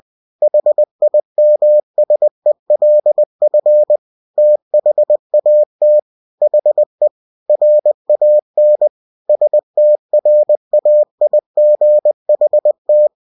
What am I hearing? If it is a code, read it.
Morse code, heard as a keyed tone, about 20 wpm, 600 Hz. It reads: HIMSELF THAT HE RAN STRAIGHT
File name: cw-books / winnie-the-pooh / winnie-20wpm-page-0043